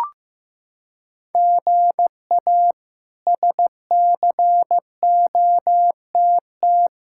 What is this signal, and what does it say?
Morse code, heard as a keyed tone, about 16 wpm, 700 Hz. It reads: GA SCOTT
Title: K1USN SST QSO Practice - 15wpm